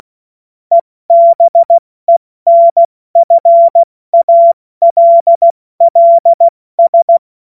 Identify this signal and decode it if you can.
Morse code, heard as a keyed tone, about 16 wpm, spaced as for 14 wpm, 680 Hz. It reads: EBENFALLS